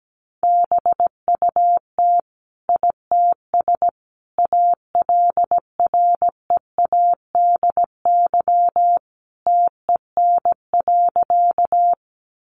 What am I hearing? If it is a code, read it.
Morse code, heard as a keyed tone, about 17 wpm, 700 Hz. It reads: BUT ITS ALREADY TEN.
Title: Sentences from Top 700 Words 17wpm